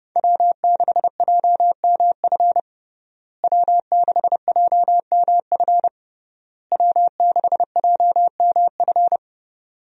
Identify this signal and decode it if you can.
Morse code, heard as a keyed tone, about 30 wpm, 700 Hz. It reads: W6JMF W6JMF W6JMF
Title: USA Call Signs - 5 Characters - Repeated 3x - 30wpm